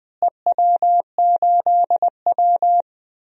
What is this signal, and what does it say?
Morse code, heard as a keyed tone, about 20 wpm, 700 Hz. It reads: EW8W
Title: DXCC-100 Call Signs - 4 Characters - Repeated 3x - 20wpm